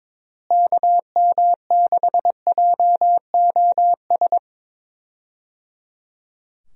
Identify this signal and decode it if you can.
Morse code, heard as a keyed tone, about 22 wpm, 700 Hz. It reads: KM6JOS
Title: USA Call Signs - 6 Characters - Repeated 3x - 22wpm